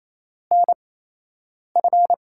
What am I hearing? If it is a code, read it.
Morse code, heard as a keyed tone, about 28 wpm, 700 Hz. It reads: N F
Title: USA Call Signs - 6 Characters - 8x Character Spacing - 28wpm